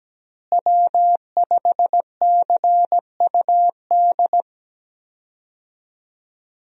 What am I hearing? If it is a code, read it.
Morse code, heard as a keyed tone, about 17 wpm, 700 Hz. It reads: W5CUD
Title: USA Call Signs - 5 Characters - Repeated 2x - 17wpm